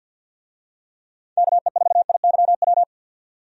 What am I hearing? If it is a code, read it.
Morse code, heard as a keyed tone, about 48 wpm, 700 Hz. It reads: KE4IYW
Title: USA Call Signs - 6 Characters - 50wpm